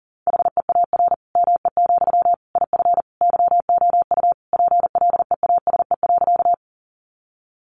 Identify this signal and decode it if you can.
Morse code, heard as a keyed tone, about 40 wpm, 700 Hz. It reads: HEAR ME, IF YOU PLEASE.
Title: Sentences from Top 1500 Words 40wpm